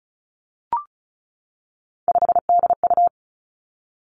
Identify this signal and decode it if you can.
Morse code, heard as a keyed tone, about 36 wpm, 700 Hz. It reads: 5DU
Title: Three Challenging Letter-Number Combinations 35wpm